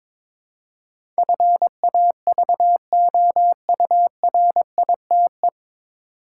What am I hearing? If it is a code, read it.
Morse code, heard as a keyed tone, about 22 wpm, 700 Hz. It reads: FAVOURITE
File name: 9 Letter Words - First 3 Letters Given - 22wpm